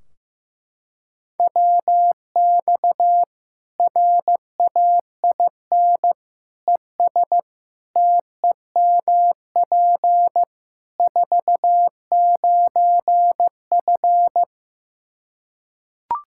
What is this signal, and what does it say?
Morse code, heard as a keyed tone, about 15 wpm, 700 Hz. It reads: WX RAIN ES TEMP 49F
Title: Practice QSO - Exchange 2 - Info - 15wpm